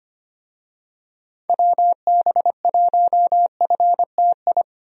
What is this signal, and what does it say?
Morse code, heard as a keyed tone, about 25 wpm, 700 Hz. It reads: WB1FTI
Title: USA Call Signs - 6 Characters - 25wpm